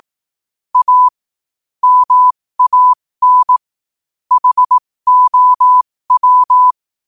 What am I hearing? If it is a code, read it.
Morse code, heard as a keyed tone, about 18 wpm, spaced as for 15 wpm, 1000 Hz. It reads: A MAN HOW
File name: qotdfiles / 15wpm